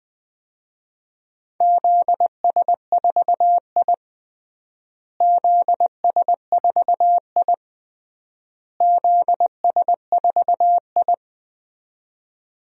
Morse code, heard as a keyed tone, about 20 wpm, 700 Hz. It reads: ZS4I ZS4I ZS4I
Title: DXCC-100 Call Signs - 4 Characters - Repeated 3x - 20wpm